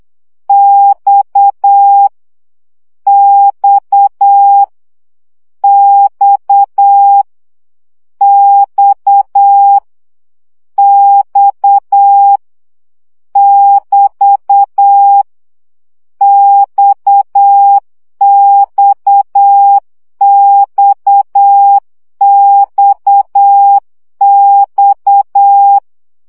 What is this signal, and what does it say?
Morse code, heard as a keyed tone, about 8 wpm, 800 Hz. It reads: X X X X X <BT> XXXXX
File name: CW gyakorlas x 7 baud